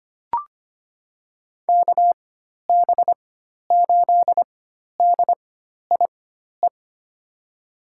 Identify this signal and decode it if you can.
Morse code, heard as a keyed tone, about 25 wpm, spaced as for 12 wpm, 700 Hz. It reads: KB8DIE
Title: USA Call Signs - 4x Character Spacing - 25wpm